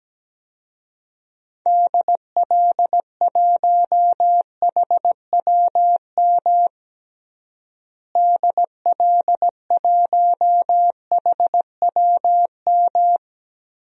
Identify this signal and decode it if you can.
Morse code, heard as a keyed tone, about 17 wpm, 700 Hz. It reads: DL1HWM DL1HWM
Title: DXCC-100 Call Signs - 6 Characters - Repeated 3x - 17wpm